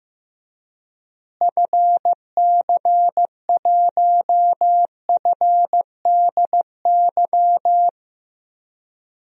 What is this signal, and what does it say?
Morse code, heard as a keyed tone, about 15 wpm, 700 Hz. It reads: FC1FDY
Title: DXCC-100 Call Signs - 6 Characters - Repeated 3x - 15wpm